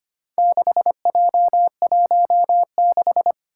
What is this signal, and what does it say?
Morse code, heard as a keyed tone, about 25 wpm, 700 Hz. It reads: 6J16
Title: Four Challenging Letter-Number Combinations 25wpm